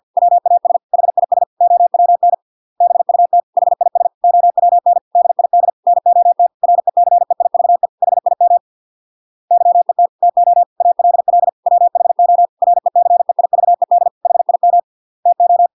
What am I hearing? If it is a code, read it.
Morse code, heard as a keyed tone, about 50 wpm, 700 Hz. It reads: WAS HIS OWN BUT HIS OWN DID NOT RECEIVE HIM YET TO ALL WHO RECEIVED HIM TO